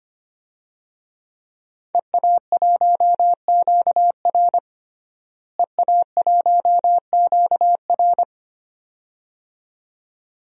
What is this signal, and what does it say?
Morse code, heard as a keyed tone, about 25 wpm, 700 Hz. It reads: EA1QR EA1QR
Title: DXCC-100 Call Signs - 5 Characters - Repeated 2x - 25wpm